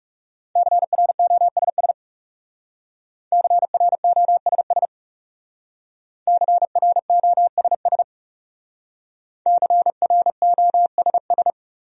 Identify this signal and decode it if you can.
Morse code, heard as a keyed tone, about 38 wpm, 700 Hz. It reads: CROSS CROSS CROSS CROSS